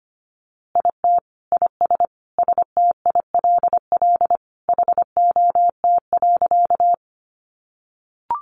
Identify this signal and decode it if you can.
Morse code, heard as a keyed tone, about 25 wpm, 700 Hz. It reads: IT IS STILL HOT.
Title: Easy Phrases 25wpm